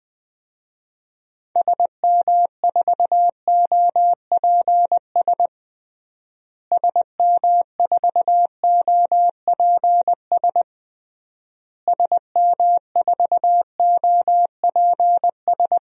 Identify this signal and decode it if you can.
Morse code, heard as a keyed tone, about 20 wpm, 700 Hz. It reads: SM4OPS SM4OPS SM4OPS
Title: DXCC-100 Call Signs - 6 Characters - Repeated 3x - 20wpm